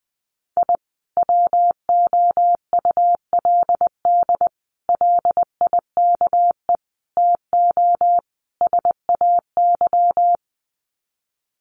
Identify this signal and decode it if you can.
Morse code, heard as a keyed tone, about 20 wpm, 700 Hz. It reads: I WOULD LIKE TO SAY
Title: Sets of 5 Words Encore2 20wpm